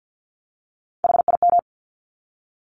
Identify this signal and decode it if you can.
Morse code, heard as a keyed tone, about 48 wpm, 700 Hz. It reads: HIM
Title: Top 200 Words 50wpm